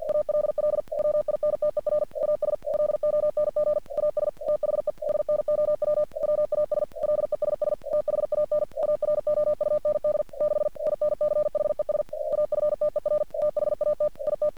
Morse code, heard as a keyed tone, about 45 wpm, 600 Hz. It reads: ALL WINTER AS LONG AS THE SNOW WAS LESS THAN AROUND 6 INCHES AFTER THAT IT
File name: archived / CRSnet